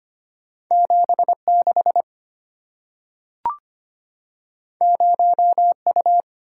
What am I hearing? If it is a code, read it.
Morse code, heard as a keyed tone, about 25 wpm, 700 Hz. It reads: 76 0U
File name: Two Letter-Number Combinations - 25wpm